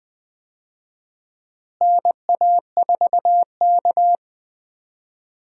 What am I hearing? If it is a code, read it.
Morse code, heard as a keyed tone, about 20 wpm, 700 Hz. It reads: NA4K
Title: USA Call Signs - 4 Characters - Repeated 2x - 20wpm